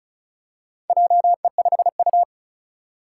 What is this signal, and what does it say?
Morse code, heard as a keyed tone, about 36 wpm, 700 Hz. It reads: JE5U
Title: Four Challenging Letter-Number Combinations 35wpm